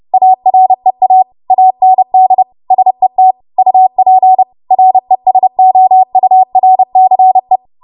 Morse code, heard as a keyed tone, about 30 wpm, 750 Hz. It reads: AREA AND SET UP RESOURCE